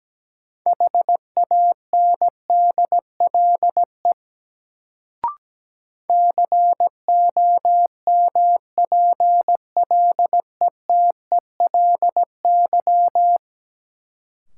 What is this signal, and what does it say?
Morse code, heard as a keyed tone, about 17 wpm, 700 Hz. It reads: HANDLE COMPLETELY
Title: Sentences from Top 1100 Words - New Word Drill Alt - 17wpm